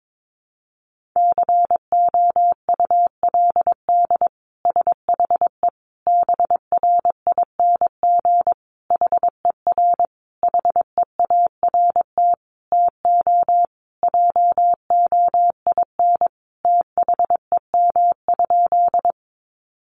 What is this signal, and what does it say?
Morse code, heard as a keyed tone, about 22 wpm, 700 Hz. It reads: COULD SHE BRING HER HEART TO JOIN THEM?
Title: Sentences from Top 1400 Words 22wpm